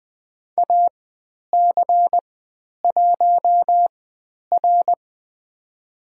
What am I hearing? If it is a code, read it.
Morse code, heard as a keyed tone, about 20 wpm, spaced as for 10 wpm, 700 Hz. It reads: AC1R
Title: 109 Lesson 18g - Letter m - Call Signs 20wpm